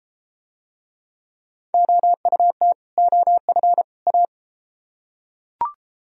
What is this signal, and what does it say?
Morse code, heard as a keyed tone, about 30 wpm, 700 Hz. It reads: OUT OF A